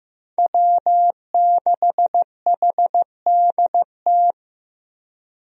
Morse code, heard as a keyed tone, about 15 wpm, 700 Hz. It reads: W6HDT
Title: USA Call Signs - 5 Characters - Repeated 3x - 15wpm